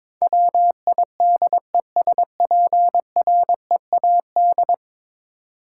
Morse code, heard as a keyed tone, about 22 wpm, 700 Hz. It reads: WIDESPREAD
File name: practice / 22wpm / 10 Letter Words - First 3 Letters Given - 22wpm